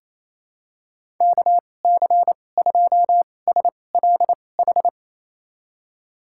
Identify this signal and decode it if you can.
Morse code, heard as a keyed tone, about 27 wpm, spaced as for 20 wpm, 700 Hz. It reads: KC2SLH
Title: USA Call Signs - 2x Character Spacing - 28wpm